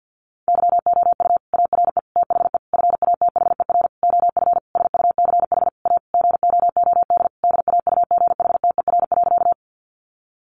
Morse code, heard as a keyed tone, about 50 wpm, 700 Hz. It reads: YOU ARE THE FATHER OF SUCH A GOOD DAUGHTER.
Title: Sentences from Top 1500 Words 50wpm